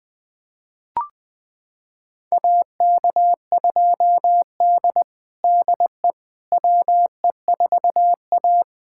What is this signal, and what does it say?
Morse code, heard as a keyed tone, about 21 wpm, 700 Hz. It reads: AK2D DE WE4A
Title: USA Callsign DE Callsign - 4 Characters - 20wpm